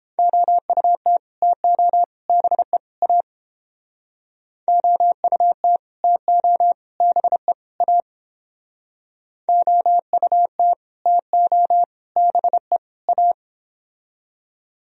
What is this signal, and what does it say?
Morse code, heard as a keyed tone, about 29 wpm, 700 Hz. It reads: OUT TO BE A OUT TO BE A OUT TO BE A